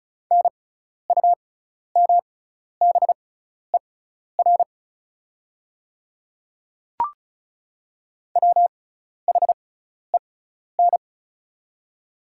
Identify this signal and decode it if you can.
Morse code, heard as a keyed tone, about 36 wpm, spaced as for 12 wpm, 700 Hz. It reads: NUMBER WHEN
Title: 6x Character Spacing - Top 100 Words - 35wpm